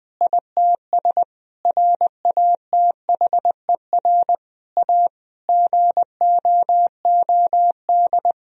Morse code, heard as a keyed tone, about 21 wpm, 700 Hz. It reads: ITS RATHER A GOOD
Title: Sentences from Top 1000 Words - Review Part 4 - 20wpm